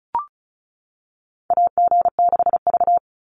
Morse code, heard as a keyed tone, about 34 wpm, 700 Hz. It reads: AG6V
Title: NA SOTA Chaser Callsigns 35wpm